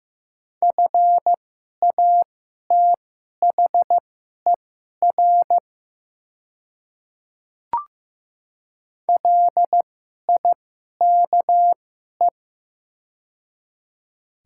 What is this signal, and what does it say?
Morse code, heard as a keyed tone, about 15 wpm, spaced as for 11 wpm, 700 Hz. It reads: FATHER LIKE
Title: 2x Character Spacing - Top 300 Words - 15wpm